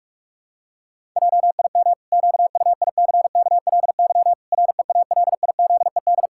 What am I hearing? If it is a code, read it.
Morse code, heard as a keyed tone, about 46 wpm, 700 Hz. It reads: JIM QUICKLY REALIZED
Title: Pangram Challenge v2 - 45wpm